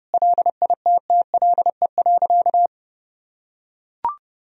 Morse code, heard as a keyed tone, about 29 wpm, 700 Hz. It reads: LITTLE.
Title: Sentences from Top 1000 Words - Review Part 1 - 30wpm